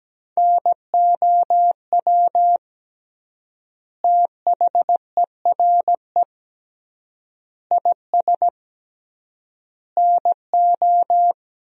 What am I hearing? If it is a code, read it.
Morse code, heard as a keyed tone, about 17 wpm, 700 Hz. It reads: NOW THERE IS NO
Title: 3x Word-Spacing - Sentences from Top 100 Words - 17wpm